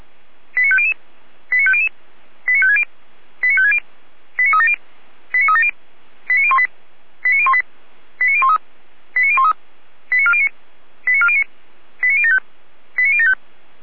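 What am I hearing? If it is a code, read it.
Five-tone selective call (ZVEI), heard as quick runs of five tones, tones of about 70 ms. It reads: 89500 89500 89560 89560 89370 89370 89018 89018 89013 89013 89409 89409 89075 89075
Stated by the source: ZVEI five-tone selective call received on SDR.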